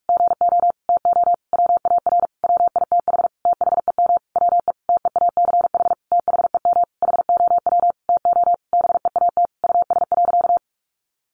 Morse code, heard as a keyed tone, about 44 wpm, 700 Hz. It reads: GO TO WAR WITH THEM WE TEACH THEM HOW TO BEAT US.
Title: Sentences from Top 1300 Words 45wpm